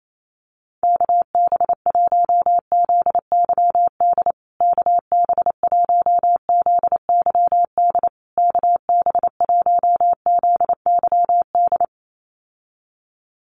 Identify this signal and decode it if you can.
Morse code, heard as a keyed tone, about 28 wpm, 700 Hz. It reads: KB1ZYD KB1ZYD KB1ZYD
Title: USA Call Signs Repeated 3x 28wpm